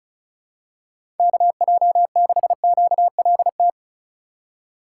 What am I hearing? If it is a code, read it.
Morse code, heard as a keyed tone, about 36 wpm, 700 Hz. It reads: KJ6QLT
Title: USA Call Signs - 6 Characters - 2x Character Spacing - 35wpm